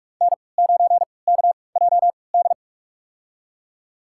Morse code, heard as a keyed tone, about 46 wpm, spaced as for 25 wpm, 700 Hz. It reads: N9KJD
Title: USA Call Signs - 3x Character Spacing - 45wpm